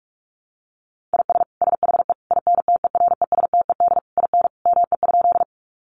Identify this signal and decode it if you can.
Morse code, heard as a keyed tone, about 44 wpm, 700 Hz. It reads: IS SHE INTERESTED IN ME?
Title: Sentences from Top 900 Words 45wpm